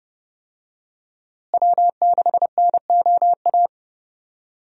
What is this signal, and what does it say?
Morse code, heard as a keyed tone, about 30 wpm, 700 Hz. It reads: W6NOA
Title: USA Call Signs - 5 Characters - Repeated 2x - 30wpm